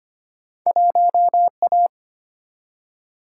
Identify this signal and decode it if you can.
Morse code, heard as a keyed tone, about 25 wpm, 700 Hz. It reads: 1A
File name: Two Letter-Number Combinations - 25wpm